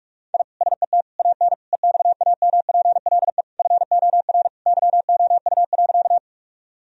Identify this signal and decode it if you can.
Morse code, heard as a keyed tone, about 46 wpm, 700 Hz. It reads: I SET AN EXAMPLE FOR YOU.
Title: Sentences from Top 1000 Words - Review Part 1 - 45wpm